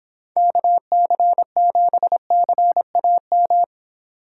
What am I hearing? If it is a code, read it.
Morse code, heard as a keyed tone, about 26 wpm, 700 Hz. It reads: KC7CAM